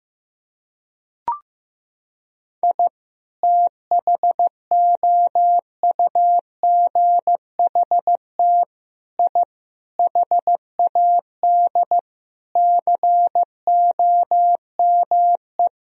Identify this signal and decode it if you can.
Morse code, heard as a keyed tone, about 15 wpm, 700 Hz. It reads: I THOUGHT I HAD COME
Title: Sentences from Top 1000 Words - Review Part 4 - 15wpm